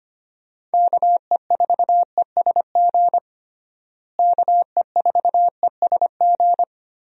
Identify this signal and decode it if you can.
Morse code, heard as a keyed tone, about 25 wpm, 700 Hz. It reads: KE4ESG KE4ESG
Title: USA Call Signs - 6 Characters - Repeated 3x - 25wpm